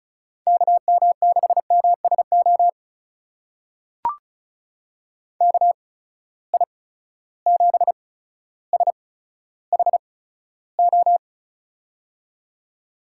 Morse code, heard as a keyed tone, about 36 wpm, spaced as for 15 wpm, 700 Hz. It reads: KM6MSO KI7SHO
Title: USA Call Signs - 6 Characters - 8x Character Spacing - 35wpm